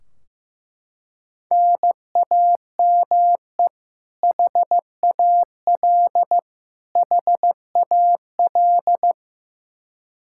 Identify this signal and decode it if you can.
Morse code, heard as a keyed tone, about 15 wpm, 700 Hz. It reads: NAME HAL HAL
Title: Practice QSO - Exchange 1 - Beginning - 15wpm